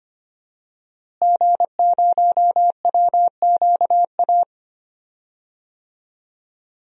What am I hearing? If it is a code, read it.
Morse code, heard as a keyed tone, about 25 wpm, 700 Hz. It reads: G0WQA
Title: DXCC-100 Call Signs - 5 Characters - Repeated 2x - 25wpm